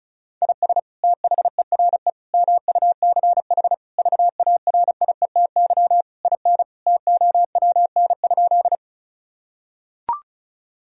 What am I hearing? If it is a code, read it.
Morse code, heard as a keyed tone, about 36 wpm, 700 Hz. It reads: IS THERE MUCH VARIETY IN TOWN?
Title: Sentences from Top 1000 Words - Review Part 4 - 35wpm